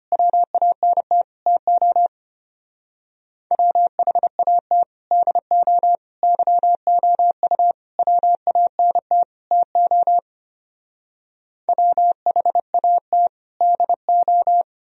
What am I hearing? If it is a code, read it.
Morse code, heard as a keyed tone, about 32 wpm, 700 Hz. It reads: WANT TO WHAT DO YOU WANT TO WHAT DO